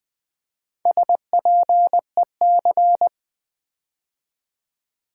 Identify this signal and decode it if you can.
Morse code, heard as a keyed tone, about 20 wpm, 700 Hz. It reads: SPEC
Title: Build A Word - 4 Letters At A Time - 12-Letter Words - 20wpm